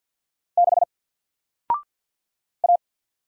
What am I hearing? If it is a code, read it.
Morse code, heard as a keyed tone, about 50 wpm, 700 Hz. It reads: 6 A
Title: Single Character (Letter:Number:Punctuation:Prosign) Alt 50wpm